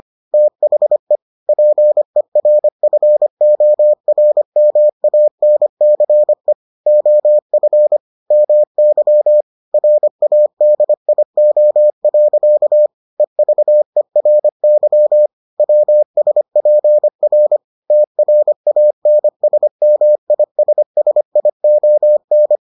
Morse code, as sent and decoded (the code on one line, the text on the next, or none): - .... / .--. .-. ..-. --- .-. -- .- -. -.-. / --- ..-. / -- -.-- / .-. .- -.. .. --- .-.-.- / . ...- . .-. -.-- / .-- ... .--. .-. / - .-. .- -. ... -- .. ... ... .. --- -.
THE PERFORMANCE OF MY RADIO. EVERY WSPR TRANSMISSION